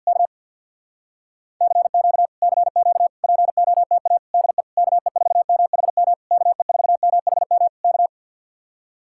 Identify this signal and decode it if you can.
Morse code, heard as a keyed tone, about 61 wpm, 700 Hz. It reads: K CQ CQ POTA DE KE4MHM KE4MHM K
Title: Speed-Racing Common QSO Phrases Encore - 50wpm